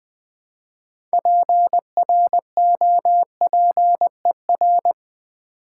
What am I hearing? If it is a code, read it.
Morse code, heard as a keyed tone, about 20 wpm, 700 Hz. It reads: PROPER
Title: Build A Word - 2 Letters At A Time - 8-Letter Words - 20wpm